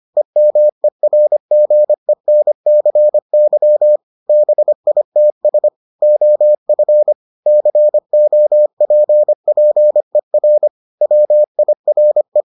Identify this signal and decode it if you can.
Morse code, heard as a keyed tone, about 25 wpm, 600 Hz. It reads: EMERGENCY BITS OF COPPER WIRE